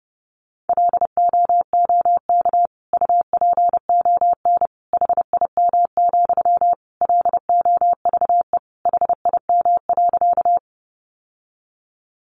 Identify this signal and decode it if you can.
Morse code, heard as a keyed tone, about 30 wpm, 700 Hz. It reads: LOOK UPON HIM, LOVE HIM.
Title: Sentences from Top 1100 Words 30wpm